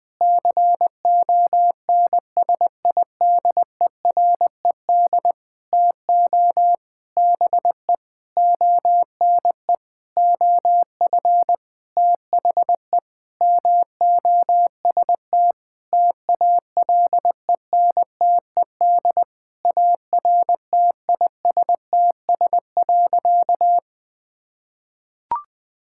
Morse code, heard as a keyed tone, about 20 wpm, 700 Hz. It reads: CONSIDERED TO BE ONE OF THE MOST TALENTED ARTISTS.
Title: Sentences with 8-Word Phrases - 20wpm